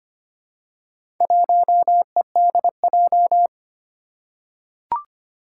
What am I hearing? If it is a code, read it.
Morse code, heard as a keyed tone, about 25 wpm, 700 Hz. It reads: 1EDJ
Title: Four Challenging Letter-Number Combinations 25wpm